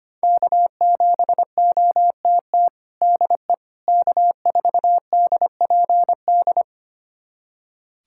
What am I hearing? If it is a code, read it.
Morse code, heard as a keyed tone, about 25 wpm, 700 Hz. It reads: K7OTT DE K4DPD
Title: USA Callsign DE Callsign - 5 Characters - 25wpm